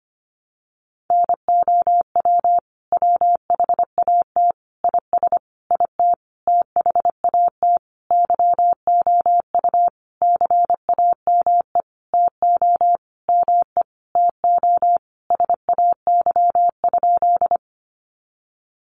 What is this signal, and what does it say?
Morse code, heard as a keyed tone, about 25 wpm, 700 Hz. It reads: NOW WHAT IS IT THAT YOU CAME TO ME TO SAY?
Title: Sentences from Top 200 Words 25wpm